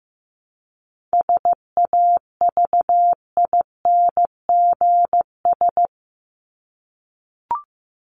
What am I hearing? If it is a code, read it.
Morse code, heard as a keyed tone, about 15 wpm, 700 Hz. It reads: SAVINGS
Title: Top 1001-1250 Words 15wpm